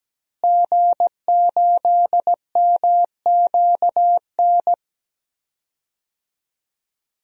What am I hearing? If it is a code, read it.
Morse code, heard as a keyed tone, about 17 wpm, 700 Hz. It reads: G8MQN
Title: DXCC-100 Call Signs - 5 Characters - Repeated 3x - 17wpm